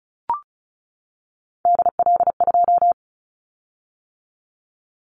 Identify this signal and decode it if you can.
Morse code, heard as a keyed tone, about 34 wpm, 700 Hz. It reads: DL2
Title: Callsign Prefixes for Top 25 DXCC - 35wpm